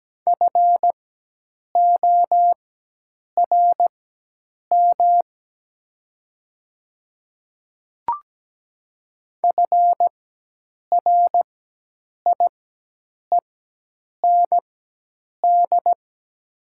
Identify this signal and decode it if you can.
Morse code, heard as a keyed tone, about 17 wpm, spaced as for 8 wpm, 690 Hz. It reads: FORM FRIEND
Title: 4x Character Spacing - Top 300 Words - 17wpm